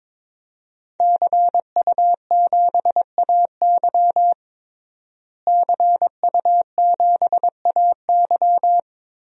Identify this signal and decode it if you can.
Morse code, heard as a keyed tone, about 22 wpm, 700 Hz. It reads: CU7AY CU7AY
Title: DXCC-100 Call Signs - 5 Characters - Repeated 3x - 22wpm